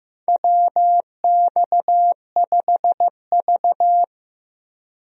Morse code, heard as a keyed tone, about 15 wpm, 700 Hz. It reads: WX5V
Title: USA Call Signs - 4 Characters - Repeated 2x - 15wpm